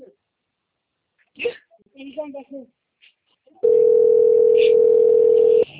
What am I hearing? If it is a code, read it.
Ringback tone (North America), 440 + 480 Hz, with speech.
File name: out- - - - - .8233